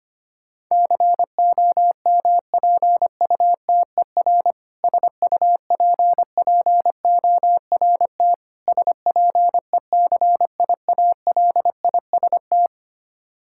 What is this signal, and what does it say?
Morse code, heard as a keyed tone, about 25 wpm, 700 Hz. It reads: COMPUTER SUPPORT SPECIALIST
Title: Practice QSO - Exchange 3 - More Info - 25wpm